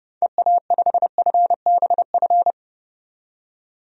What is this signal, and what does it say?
Morse code, heard as a keyed tone, about 32 wpm, 690 Hz. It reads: EA5FBF
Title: DXCC-100 Call Signs - 6 Characters - 30wpm